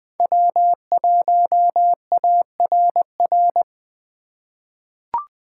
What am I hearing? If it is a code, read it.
Morse code, heard as a keyed tone, about 21 wpm, 700 Hz. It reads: W1ARR
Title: 2023 POTA Hunter Call Signs - 20wpm